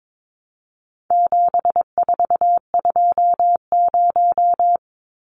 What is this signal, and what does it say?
Morse code, heard as a keyed tone, about 22 wpm, 700 Hz. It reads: 7420
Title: Numbers - 4-Digit 22wpm